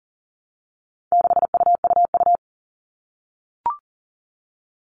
Morse code, heard as a keyed tone, about 40 wpm, 700 Hz. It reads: 6UUU
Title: All Cut Numbers - 4-Digit 40wpm